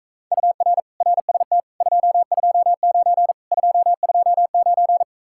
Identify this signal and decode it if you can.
Morse code, heard as a keyed tone, about 45 wpm, 700 Hz. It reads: UR RST 229 229